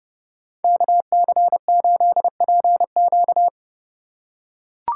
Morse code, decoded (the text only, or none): KC8PQ